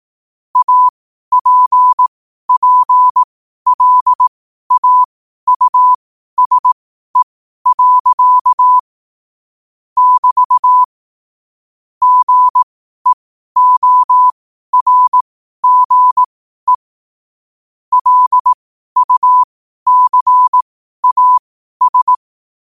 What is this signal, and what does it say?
Morse code, heard as a keyed tone, about 18 wpm, spaced as for 12 wpm, 1000 Hz. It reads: APPLAUSE. <BT> GEORGE LUCAS
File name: qotdfiles / 12wpm